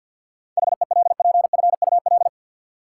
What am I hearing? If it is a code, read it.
Morse code, heard as a keyed tone, about 53 wpm, 700 Hz. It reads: HELPFUL
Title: Sentences from Top 1000 Words - Review Part 6 - New Word Drill - 50wpm